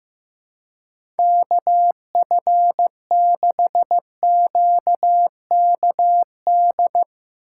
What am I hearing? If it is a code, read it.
Morse code, heard as a keyed tone, about 15 wpm, 700 Hz. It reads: KF6QKD
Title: USA Call Signs - 6 Characters - 2x Character Spacing - 15wpm